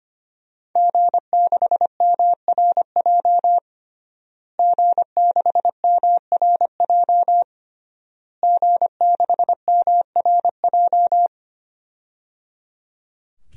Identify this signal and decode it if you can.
Morse code, heard as a keyed tone, about 25 wpm, 700 Hz. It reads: G6MRJ G6MRJ G6MRJ
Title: DXCC-100 Call Signs - 5 Characters - Repeated 3x - 25wpm